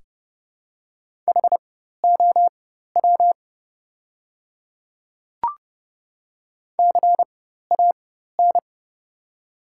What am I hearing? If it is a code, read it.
Morse code, heard as a keyed tone, about 31 wpm, spaced as for 14 wpm, 700 Hz. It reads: HOW CAN
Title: 4x Character Spacing - Top 200 Words - 30wpm